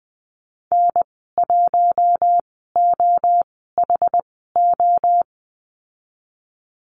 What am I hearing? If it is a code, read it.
Morse code, heard as a keyed tone, about 20 wpm, spaced as for 15 wpm, 700 Hz. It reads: N1OHO
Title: 076 Lesson 13g - Letter L - Call Signs 20wpm